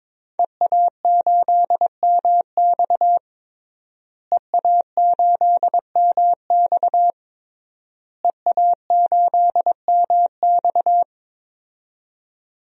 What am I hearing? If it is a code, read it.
Morse code, heard as a keyed tone, about 22 wpm, 700 Hz. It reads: EA8MX EA8MX EA8MX
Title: DXCC-100 Call Signs - 5 Characters - Repeated 3x - 22wpm